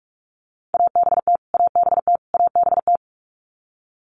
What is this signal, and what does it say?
Morse code, heard as a keyed tone, about 46 wpm, 700 Hz. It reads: ABT ABT ABT
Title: Abbreviations Repeated 3x - 45wpm